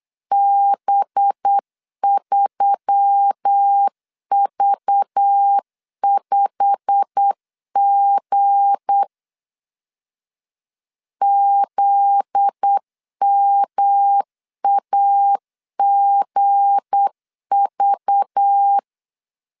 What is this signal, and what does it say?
Morse code, heard as a keyed tone, about 8 wpm, 800 Hz. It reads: B3V5G ZMAGV
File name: LEZIONI / 8 YJ45 / 8 YJ45 8wpm 3214